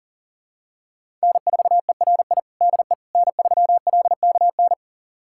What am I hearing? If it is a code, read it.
Morse code, heard as a keyed tone, about 40 wpm, 700 Hz. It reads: N4ERI DE N3LKN
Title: USA Callsign DE Callsign - 5 Characters - 40wpm